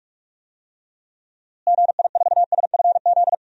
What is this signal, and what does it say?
Morse code, heard as a keyed tone, about 41 wpm, 700 Hz. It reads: GI4SFZ
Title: DXCC-100 Call Signs - 6 Characters - Repeated 2x - 45wpm